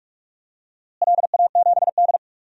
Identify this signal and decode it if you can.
Morse code, heard as a keyed tone, about 46 wpm, 700 Hz. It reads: LA7D
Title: DXCC-100 Call Signs - 4 Characters - Repeated 2x - 45wpm